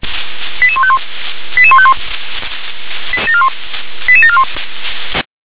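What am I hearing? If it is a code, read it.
Five-tone selective call (ZVEI), heard as quick runs of five tones, tones of about 70 ms. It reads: 88151 88151 88741
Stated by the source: ZVEI five-tone call received with an SDR.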